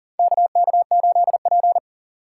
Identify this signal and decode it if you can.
Morse code, heard as a keyed tone, about 37 wpm, 700 Hz. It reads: KK8P
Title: USA Call Signs - 4 Characters - 6x Character Spacing - 40wpm